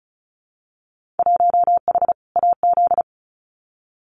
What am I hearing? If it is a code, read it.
Morse code, heard as a keyed tone, about 36 wpm, 700 Hz. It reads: 1H AZ
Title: Winter Field Day Exchanges 35wpm